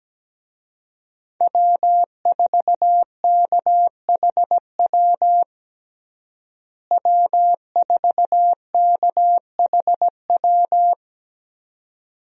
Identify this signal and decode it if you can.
Morse code, heard as a keyed tone, about 17 wpm, 700 Hz. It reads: W4KHW W4KHW
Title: USA Call Signs - 5 Characters - Repeated 2x - 17wpm